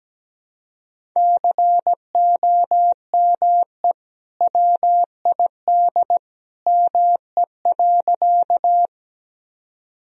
Morse code, heard as a keyed tone, about 17 wpm, 700 Hz. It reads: COME WID ME.
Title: Sentences from Top 200 Words and Abbreviations 17wpm